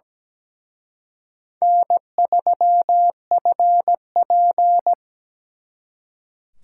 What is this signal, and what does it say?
Morse code, heard as a keyed tone, about 20 wpm, 700 Hz. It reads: N3FP
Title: Speed Racing - USA 1x2 Call Signs 17wpm